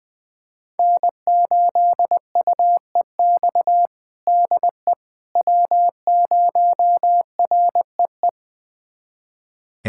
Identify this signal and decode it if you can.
Morse code, heard as a keyed tone, about 21 wpm, 700 Hz. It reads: N8UEX DE W0REE
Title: USA Callsign DE Callsign - 5 Characters - 20wpm